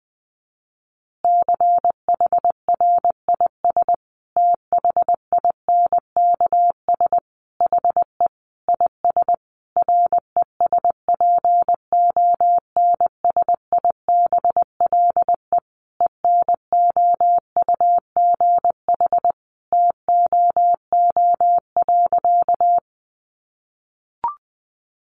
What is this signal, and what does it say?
Morse code, heard as a keyed tone, about 20 wpm, 700 Hz. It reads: CHRIS THINKS HE IS RESPONSIBLE ENOUGH TOO.
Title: Easy Phrases 20wpm